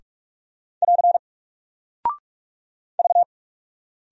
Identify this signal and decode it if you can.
Morse code, heard as a keyed tone, about 46 wpm, 700 Hz. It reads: <AR> V
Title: Single Character (Letter:Number:Punctuation:Prosign) Alt 45wpm